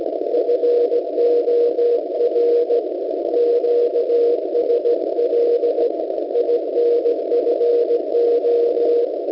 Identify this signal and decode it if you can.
Morse code, heard as a keyed tone, about 17 wpm, 480 Hz. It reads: FOR QSL INFO